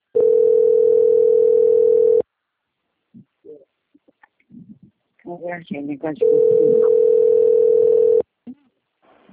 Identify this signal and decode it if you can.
Ringback tone (North America), 440 + 480 Hz, with speech.
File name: out- - - - - .14562